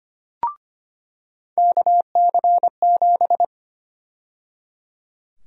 Morse code, heard as a keyed tone, about 25 wpm, 700 Hz. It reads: KC7
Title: USA Call Sign Prefixes - 25wpm